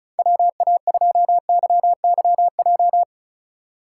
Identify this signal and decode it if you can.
Morse code, heard as a keyed tone, about 34 wpm, 700 Hz. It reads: WA2YYJ
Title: USA Call Signs - 6 Characters - 3x Character Spacing - 35wpm